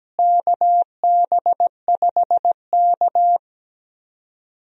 Morse code, heard as a keyed tone, about 17 wpm, 700 Hz. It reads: KB5K
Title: USA Call Signs - 4 Characters - Repeated 3x - 17wpm